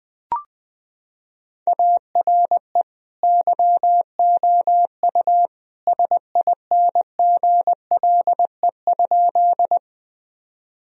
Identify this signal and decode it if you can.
Morse code, heard as a keyed tone, about 21 wpm, 700 Hz. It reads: ARE YOU SINGLE?
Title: Sentences from Top 1000 Words - Review Part 4 - 20wpm